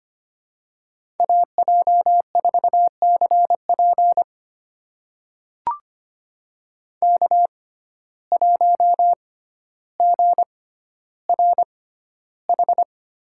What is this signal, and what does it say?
Morse code, heard as a keyed tone, about 25 wpm, spaced as for 13 wpm, 700 Hz. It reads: AJ4CP K1GRH
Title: USA Call Signs - 5 Characters - 6x Character Spacing - 25wpm